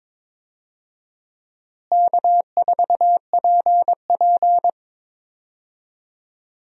Morse code, heard as a keyed tone, about 22 wpm, 700 Hz. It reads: K4PP
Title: USA Call Signs - 4 Characters - Repeated 3x - 22wpm